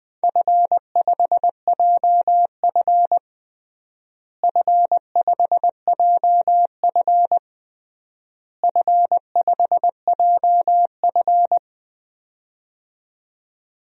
Morse code, heard as a keyed tone, about 20 wpm, 700 Hz. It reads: F5JF F5JF F5JF
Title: DXCC-100 Call Signs - 4 Characters - Repeated 3x - 20wpm